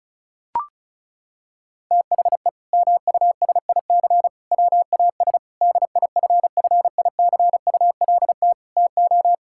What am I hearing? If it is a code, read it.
Morse code, heard as a keyed tone, about 34 wpm, 700 Hz. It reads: THE MUSIC WAS DIFFICULT TO
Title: Sentences from Top 1000 Words - Review Part 6 - 35wpm